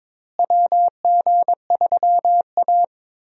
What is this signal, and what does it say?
Morse code, heard as a keyed tone, about 22 wpm, 700 Hz. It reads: WG3A
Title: USA Call Signs - 4 Characters - Repeated 3x - 22wpm